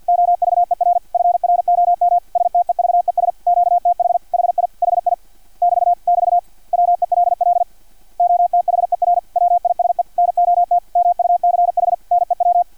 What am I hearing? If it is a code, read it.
Morse code, heard as a keyed tone, about 50 wpm, 700 Hz. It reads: OVER FROM STEVES QTH HI HI <BT> <BT> WELL OTHER WISE NOT MUCH NEW
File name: archived / CRSnet